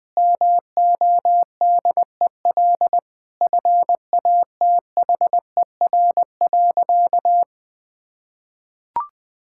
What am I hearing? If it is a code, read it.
Morse code, heard as a keyed tone, about 20 wpm, 700 Hz. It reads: MODEL FATHER.
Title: Sentences from Top 1000 Words - Review Part 5 - 20wpm